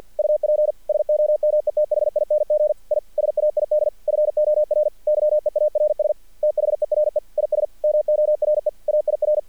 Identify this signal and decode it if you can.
Morse code, heard as a keyed tone, about 48 wpm, 600 Hz. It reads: UP SOMETHING I SAID FOR YEARS THERE IS MORE AIR
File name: archived / CRSnet